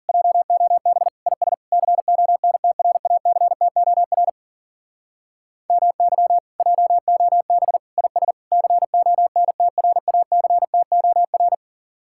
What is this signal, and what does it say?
Morse code, heard as a keyed tone, about 48 wpm, 700 Hz. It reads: JOB IS CONTRACTOR MY JOB IS CONTRACTOR
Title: Speed-Racing Common QSO Phrases - 40wpm